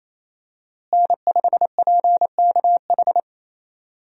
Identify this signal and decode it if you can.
Morse code, heard as a keyed tone, about 28 wpm, 700 Hz. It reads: N5PKH
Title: USA Call Signs - 5 Characters - Repeated 2x - 28wpm